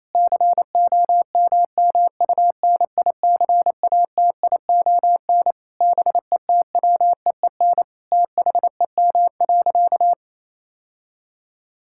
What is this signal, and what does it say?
Morse code, heard as a keyed tone, about 28 wpm, 700 Hz. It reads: COMMUNICATION BETWEEN THEM.
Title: Sentences from Top 1000 Words - Review Part 6 - 28wpm